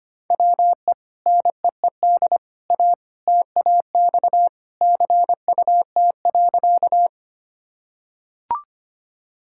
Morse code, heard as a keyed tone, about 25 wpm, 700 Hz. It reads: WE NEED A TAX CUT.
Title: Sentences from Top 1000 Words - Review Part 5 - 25wpm